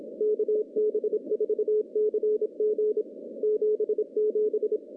Morse code, heard as a keyed tone, about 29 wpm, 410 Hz. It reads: KB4CG 77
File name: contests / CW-SS-k5tr- -0200